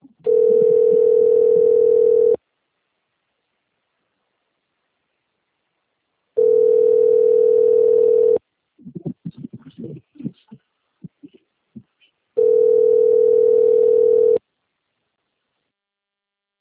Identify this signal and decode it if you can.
Ringback tone (North America), 440 + 480 Hz.